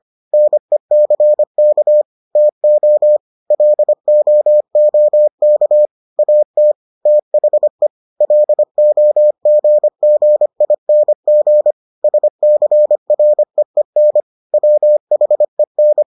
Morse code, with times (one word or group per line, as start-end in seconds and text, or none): 0.3-2.0 NECK
2.4-3.2 TO
3.5-5.9 LOOK
6.2-6.7 AT
7.1-7.9 THE
8.2-11.7 LOGGING
12.0-14.2 SCREEN
14.5-16.0 WHEN